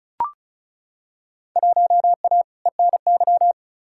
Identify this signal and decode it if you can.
Morse code, heard as a keyed tone, about 34 wpm, 700 Hz. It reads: 1A ENY
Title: Field Day Exchanges - 35wpm